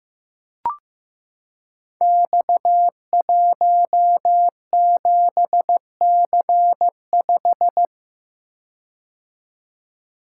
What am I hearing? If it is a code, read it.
Morse code, heard as a keyed tone, about 15 wpm, 700 Hz. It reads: X17C5
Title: Code Groups - 5 Letter:Numbers - 15wpm